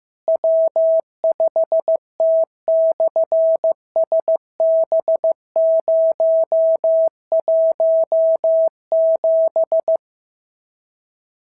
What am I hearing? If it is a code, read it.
Morse code, heard as a keyed tone, about 15 wpm, 650 Hz. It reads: W5T/SB017
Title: SOTA QSO Practice 15wpm